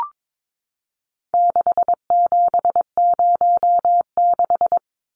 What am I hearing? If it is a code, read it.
Morse code, heard as a keyed tone, about 22 wpm, 700 Hz. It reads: 6706
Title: Numbers - 4-Digit 22wpm